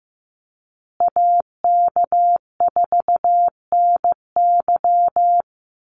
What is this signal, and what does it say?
Morse code, heard as a keyed tone, about 15 wpm, 700 Hz. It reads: AK4NY
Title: Call Signs Repeated 2x - 15wpm